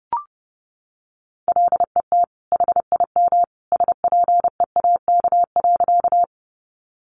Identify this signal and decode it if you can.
Morse code, heard as a keyed tone, about 30 wpm, 700 Hz. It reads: LET HIM SPEAK.
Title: Sentences from Top 1100 Words 30wpm